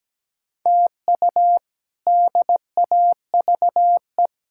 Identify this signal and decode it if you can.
Morse code, heard as a keyed tone, about 17 wpm, 700 Hz. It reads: TU DAVE
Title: K1USN SST QSO Practice - 17wpm